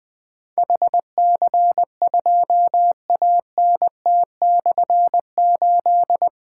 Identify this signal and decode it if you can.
Morse code, heard as a keyed tone, about 20 wpm, 700 Hz. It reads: HC2ANT/8
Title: Difficult Call Signs 20wpm